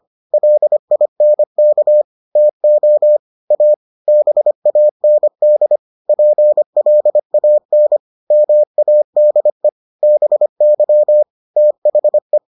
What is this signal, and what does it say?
Morse code, heard as a keyed tone, about 25 wpm, 600 Hz. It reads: LINK TO A BAND PLAN MADE BY THE